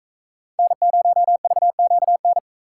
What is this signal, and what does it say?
Morse code, heard as a keyed tone, about 44 wpm, 700 Hz. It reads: N0VQN